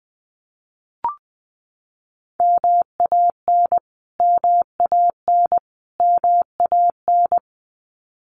Morse code, heard as a keyed tone, about 20 wpm, 700 Hz. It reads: MAN MAN MAN
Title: Top 200 Words Repeated 20wpm